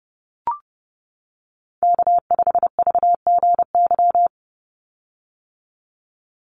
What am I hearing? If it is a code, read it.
Morse code, heard as a keyed tone, about 30 wpm, 700 Hz. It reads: K5VGY
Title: USA Call Signs 30wpm